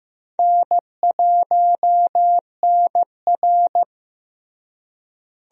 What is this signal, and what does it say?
Morse code, heard as a keyed tone, about 15 wpm, 700 Hz. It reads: N1NR
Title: USA Call Signs - 4 Characters - Repeated 2x - 15wpm